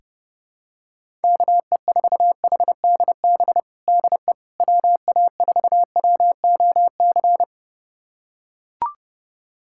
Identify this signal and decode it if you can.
Morse code, heard as a keyed tone, about 30 wpm, 700 Hz. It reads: KE4HDB DE WA4WOC
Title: USA Callsign DE Callsign - 6 Characters - 30wpm